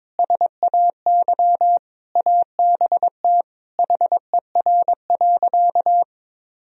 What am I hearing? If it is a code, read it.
Morse code, heard as a keyed tone, about 22 wpm, 700 Hz. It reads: SAY ABT HER.
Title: Sentences from Top 200 Words and Abbreviations 22wpm